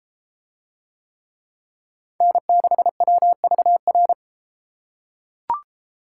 Morse code, heard as a keyed tone, about 30 wpm, 700 Hz. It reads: N6WVR